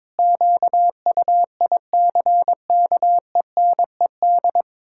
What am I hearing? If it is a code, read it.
Morse code, heard as a keyed tone, about 22 wpm, 700 Hz. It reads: QUICKENED
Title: Pangram Challenge v2 - 22wpm